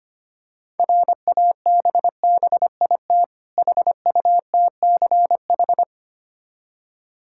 Morse code, heard as a keyed tone, about 25 wpm, 690 Hz. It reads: RABBIT HUTCH
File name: practice / 25wpm / Difficult Words with Hints - 25wpm